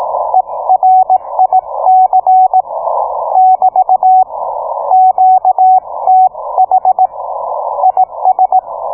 Morse code, heard as a keyed tone, about 18 wpm, 750 Hz. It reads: ERIC <BT> QTH IS